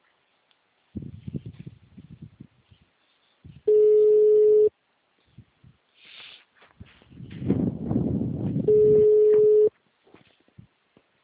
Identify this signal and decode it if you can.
Ringback tone (Europe, 425 Hz), with speech.